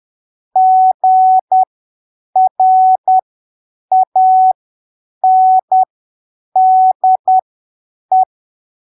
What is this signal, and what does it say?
Morse code, heard as a keyed tone, about 10 wpm, spaced as for 7 wpm, 750 Hz. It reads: GRANDE
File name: arquivos / texto / texto 22 10ppm